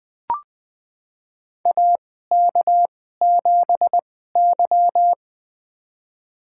Morse code, heard as a keyed tone, about 20 wpm, spaced as for 15 wpm, 700 Hz. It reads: AK7Y
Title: USA Call Signs - 4 Characters - 2x Character Spacing - 20wpm